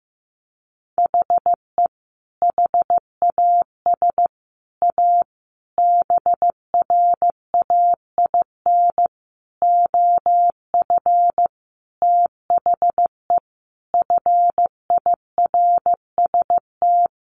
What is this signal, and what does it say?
Morse code, heard as a keyed tone, about 15 wpm, 700 Hz. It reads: HE HAS A BRAIN OF THE FIRST
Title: Sentences from Top 1100 Words 15wpm